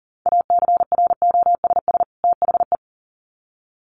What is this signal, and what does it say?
Morse code, heard as a keyed tone, about 38 wpm, 700 Hz. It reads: ACROSS THE
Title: Sets of 2 Words 40wpm